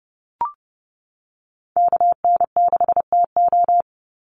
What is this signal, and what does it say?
Morse code, heard as a keyed tone, about 30 wpm, 700 Hz. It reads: KN6TO
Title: USA Call Signs 30wpm